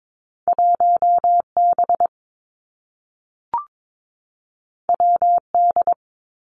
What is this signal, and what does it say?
Morse code, heard as a keyed tone, about 22 wpm, 700 Hz. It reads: 1 B W D
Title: Two Challenging Letter-Number Combinations 22wpm